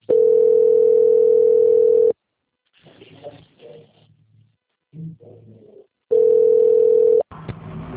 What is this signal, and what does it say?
Ringback tone (North America), 440 + 480 Hz.